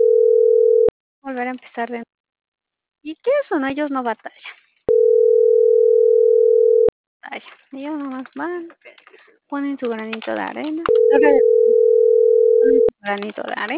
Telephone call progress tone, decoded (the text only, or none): ringback tone